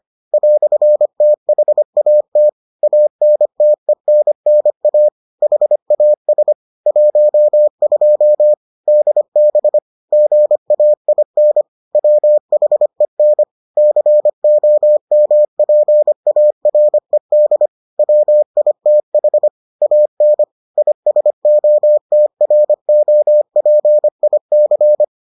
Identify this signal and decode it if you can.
Morse code, heard as a keyed tone, about 25 wpm, 600 Hz. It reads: "THAT ANTENNA HAS 12 DB GAIN WHEN COMPARED WITH AN ISOTROPIC